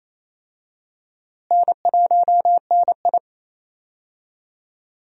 Morse code, heard as a keyed tone, about 28 wpm, 700 Hz. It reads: N1NI
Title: USA Call Signs - 4 Characters - Repeated 2x - 28wpm